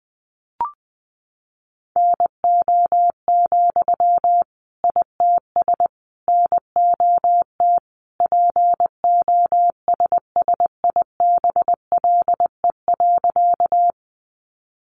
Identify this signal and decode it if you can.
Morse code, heard as a keyed tone, about 20 wpm, 700 Hz. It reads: NO, ITS NOT POSSIBLE.
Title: Sentences from Top 600 Words 20wpm